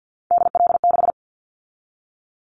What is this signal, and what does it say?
Morse code, heard as a keyed tone, about 50 wpm, 700 Hz. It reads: DL6
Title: Callsign Prefixes for Top 50 DXCC - 50wpm